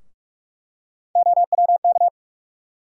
Morse code, heard as a keyed tone, about 42 wpm, 700 Hz. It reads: OWK
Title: USA Call Sign Postfixes - 3 Letters 45wpm